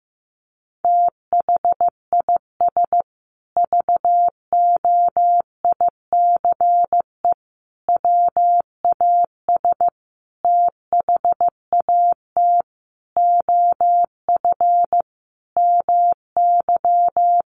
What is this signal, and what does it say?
Morse code, heard as a keyed tone, about 15 wpm, 700 Hz. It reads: THIS VOICE WAS THAT OF MY
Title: Sentences from Top 500 Words 15wpm